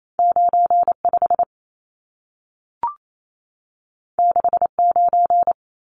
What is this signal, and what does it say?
Morse code, heard as a keyed tone, about 28 wpm, 700 Hz. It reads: 95 69
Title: Numbers - 2-Digit 28wpm